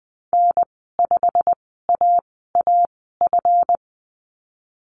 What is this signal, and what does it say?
Morse code, heard as a keyed tone, about 20 wpm, spaced as for 14 wpm, 700 Hz. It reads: N5AAF
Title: 154 Lesson 25g - Letter p - Call Signs 20wpm